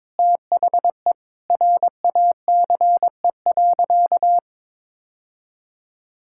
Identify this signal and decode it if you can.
Morse code, heard as a keyed tone, about 22 wpm, 700 Hz. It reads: THE RACE.
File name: practice / 22wpm / Sentences with 8-Word Phrases - 22wpm